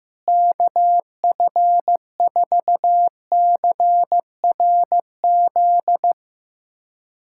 Morse code, heard as a keyed tone, about 15 wpm, 700 Hz. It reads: KF4CRZ
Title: USA Callsign DE Callsign - 6 Characters - 15wpm